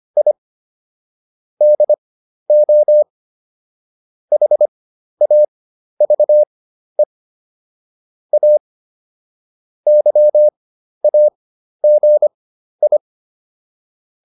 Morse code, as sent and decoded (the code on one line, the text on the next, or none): .. / -.. --- / .... .- ...- . / .- / -.-- .- --. ..
I DO HAVE A YAGI